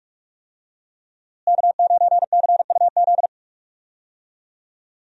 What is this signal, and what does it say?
Morse code, heard as a keyed tone, about 45 wpm, 700 Hz. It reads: K9CUZ
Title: USA Call Signs - 5 Characters - Repeated 2x - 45wpm